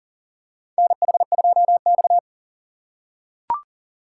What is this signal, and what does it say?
Morse code, heard as a keyed tone, about 40 wpm, 700 Hz. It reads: NH2X
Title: USA Call Signs - 4 Characters - 3x Character Spacing - 40wpm